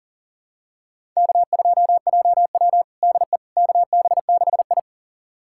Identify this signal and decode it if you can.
Morse code, heard as a keyed tone, about 40 wpm, 700 Hz. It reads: K2JW DE KB6I
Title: USA Callsign DE Callsign - 4 Characters - 40wpm